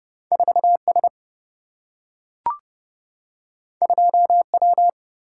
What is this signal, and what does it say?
Morse code, heard as a keyed tone, about 30 wpm, 700 Hz. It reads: 4S 2W
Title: Single-Letter Single-Number 30wpm